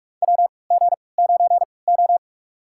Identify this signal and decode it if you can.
Morse code, heard as a keyed tone, about 42 wpm, spaced as for 25 wpm, 690 Hz. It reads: WG9O
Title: USA Call Signs - 4 Characters - 3x Character Spacing - 45wpm